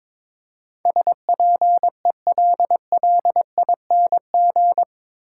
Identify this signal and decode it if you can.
Morse code, heard as a keyed tone, about 21 wpm, 700 Hz. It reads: SPELLING
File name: practice / 22wpm / Eight Letter Words - 22wpm